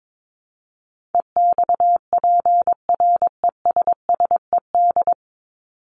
Morse code, heard as a keyed tone, about 22 wpm, 700 Hz. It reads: EXPRESSED
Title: ED Suffix - Top 100 Words with Base Words 22wpm